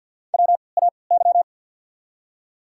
Morse code, heard as a keyed tone, about 54 wpm, spaced as for 29 wpm, 700 Hz. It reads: WAY
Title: 3x Character Spacing - Top 200 Words - 50wpm